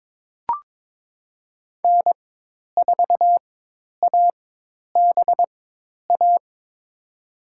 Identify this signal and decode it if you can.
Morse code, heard as a keyed tone, about 21 wpm, spaced as for 10 wpm, 700 Hz. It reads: N4ABA
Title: USA Call Signs - 5 Characters - 4x Character Spacing - 22wpm